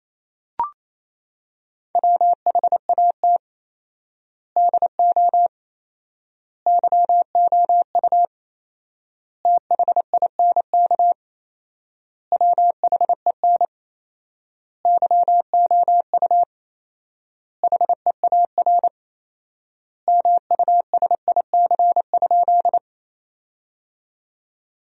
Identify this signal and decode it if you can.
Morse code, heard as a keyed tone, about 28 wpm, 700 Hz. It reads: WHAT DO YOU THINK WHEN YOU HEAR MUSIC?
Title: 4x Word-Spacing - Sentences from Top 300 Words - 28wpm